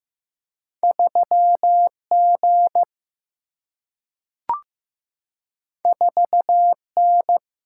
Morse code, heard as a keyed tone, about 15 wpm, 700 Hz. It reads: 3G 4N
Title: Single-Letter Single-Number 15wpm